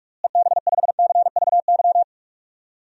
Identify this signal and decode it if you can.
Morse code, heard as a keyed tone, about 46 wpm, 700 Hz. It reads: EB5CVY
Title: DXCC-100 Call Signs - 6 Characters - Repeated 2x - 45wpm